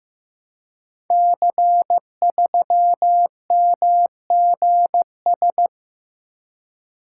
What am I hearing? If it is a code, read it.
Morse code, heard as a keyed tone, about 15 wpm, 700 Hz. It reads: C3MGS
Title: Code Groups - 5 Letter:Numbers - 15wpm